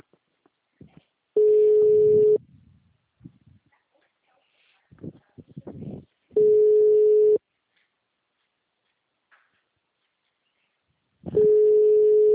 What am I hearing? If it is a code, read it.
Ringback tone (Europe, 425 Hz).